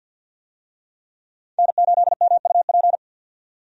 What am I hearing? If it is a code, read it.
Morse code, heard as a keyed tone, about 47 wpm, 700 Hz. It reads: N8MUP
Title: USA Call Signs - 5 Characters - 50wpm